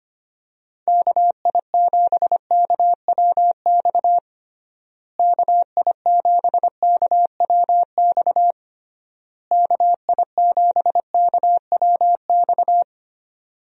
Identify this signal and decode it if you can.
Morse code, heard as a keyed tone, about 25 wpm, 700 Hz. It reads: KI7KWX KI7KWX KI7KWX
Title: USA Call Signs - 6 Characters - Repeated 3x - 25wpm